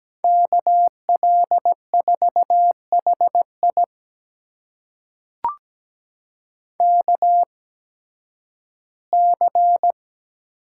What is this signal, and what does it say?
Morse code, heard as a keyed tone, about 17 wpm, spaced as for 11 wpm, 700 Hz. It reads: KL4HI KC
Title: USA Call Signs - 5 Characters - 8x Character Spacing - 17wpm